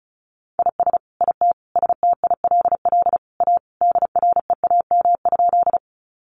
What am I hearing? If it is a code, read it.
Morse code, heard as a keyed tone, about 34 wpm, 700 Hz. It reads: IS IT STILL A DREAM?
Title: Sentences from Top 1500 Words 35wpm